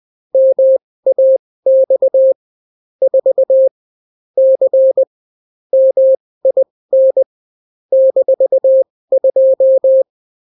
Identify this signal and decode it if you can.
Morse code, heard as a keyed tone, about 20 wpm, spaced as for 16 wpm, 530 Hz. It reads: MAX 4 C MIN -2